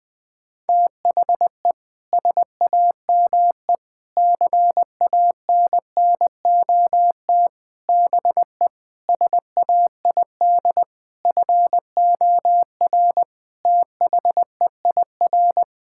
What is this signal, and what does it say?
Morse code, heard as a keyed tone, about 20 wpm, 700 Hz. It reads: THE SAME CANNOT BE SAID FOR THEIR
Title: Sentences with 7-Word Phrases - 20wpm